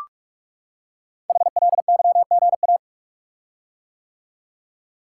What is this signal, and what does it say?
Morse code, heard as a keyed tone, about 44 wpm, 700 Hz. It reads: HLYGA
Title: Code Groups - 5 Letters - 45wpm